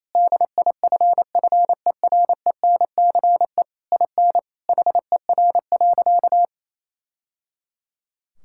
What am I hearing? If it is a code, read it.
Morse code, heard as a keyed tone, about 28 wpm, 700 Hz. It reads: DIFFERENCE IN HER.
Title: Sentences from Top 1000 Words - Review Part 5 - 28wpm